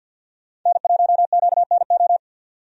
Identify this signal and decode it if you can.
Morse code, heard as a keyed tone, about 50 wpm, 700 Hz. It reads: N1QNO
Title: USA Call Signs - 5 Characters - 50wpm